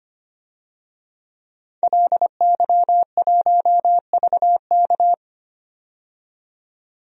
Morse code, heard as a keyed tone, about 25 wpm, 700 Hz. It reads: LY1VK
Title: DXCC-100 Call Signs - 5 Characters - 25wpm